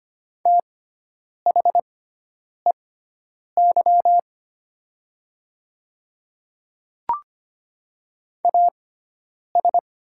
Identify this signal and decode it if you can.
Morse code, heard as a keyed tone, about 25 wpm, spaced as for 9 wpm, 700 Hz. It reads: THEY AS
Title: 6x Character Spacing - Top 300 Words - 25wpm